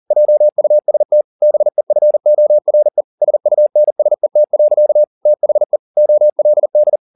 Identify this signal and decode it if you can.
Morse code, heard as a keyed tone, about 40 wpm, 600 Hz. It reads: JUST BEFORE SUNSET. THE OLD
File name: the old man and the sea 2 40wpm